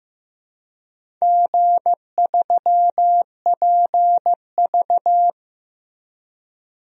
Morse code, heard as a keyed tone, about 15 wpm, 700 Hz. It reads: G3PV
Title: DXCC-100 Call Signs - 4 Characters - Repeated 3x - 15wpm